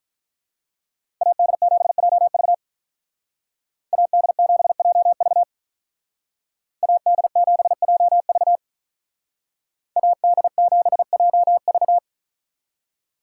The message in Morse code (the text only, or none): AD7JV AD7JV AD7JV AD7JV